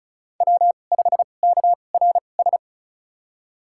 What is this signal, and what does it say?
Morse code, heard as a keyed tone, about 36 wpm, spaced as for 25 wpm, 700 Hz. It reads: W5KRS
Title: USA Call Signs - 2x Character Spacing - 35wpm